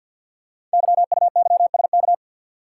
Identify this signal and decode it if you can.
Morse code, heard as a keyed tone, about 48 wpm, 690 Hz. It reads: YUYSK
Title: Code Groups - 5 Letters - 50wpm